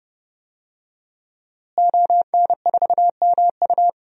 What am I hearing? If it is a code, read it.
Morse code, heard as a keyed tone, about 30 wpm, 700 Hz. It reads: ON4MU
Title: DXCC-100 Call Signs - 5 Characters - Repeated 3x - 30wpm